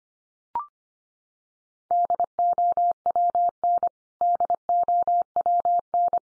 Morse code, heard as a keyed tone, about 25 wpm, 700 Hz. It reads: DOWN DOWN
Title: Top 100 Words 25wpm x3